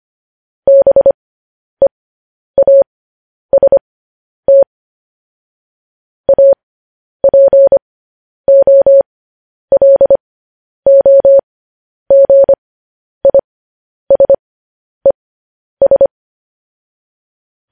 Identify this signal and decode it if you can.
Morse code, heard as a keyed tone, about 25 wpm, spaced as for 10 wpm, 550 Hz. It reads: BEAST APOLOGISES
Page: News Headlines in Morse Code at 10 WPM